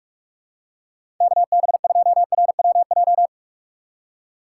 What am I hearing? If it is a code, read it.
Morse code, heard as a keyed tone, about 47 wpm, 700 Hz. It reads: KB2RWJ
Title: USA Call Signs - 6 Characters - 6x Character Spacing - 45wpm